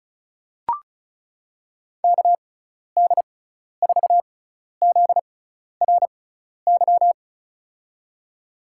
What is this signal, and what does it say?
Morse code, heard as a keyed tone, about 36 wpm, spaced as for 12 wpm, 690 Hz. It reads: KD4ZRY
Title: USA Call Signs - 6 Characters - 6x Character Spacing - 35wpm